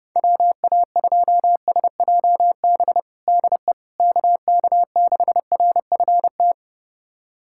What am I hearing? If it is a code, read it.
Morse code, heard as a keyed tone, about 30 wpm, 700 Hz. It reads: WA2SJB DE KK6RFT
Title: USA Callsign DE Callsign - 6 Characters - 30wpm